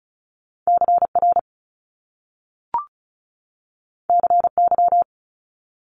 Morse code, heard as a keyed tone, about 36 wpm, 700 Hz. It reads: CR CY
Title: Top Two Letter Combinations in Words 35wpm